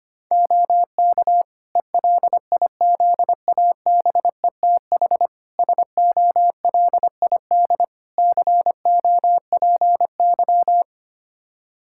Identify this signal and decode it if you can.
Morse code, heard as a keyed tone, about 25 wpm, 700 Hz. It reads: OK ELIZABETH SOLID COPY
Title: Practice QSO - Exchange 3 - More Info - 25wpm